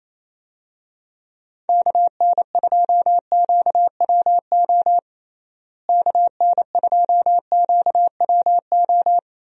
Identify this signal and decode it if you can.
Morse code, heard as a keyed tone, about 28 wpm, 700 Hz. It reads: KN2QWO KN2QWO
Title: USA Call Signs - 6 Characters - Repeated 2x - 28wpm